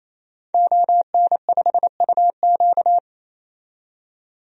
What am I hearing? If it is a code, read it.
Morse code, heard as a keyed tone, about 28 wpm, 700 Hz. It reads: ON5UQ
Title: DXCC-100 Call Signs - 5 Characters - 28wpm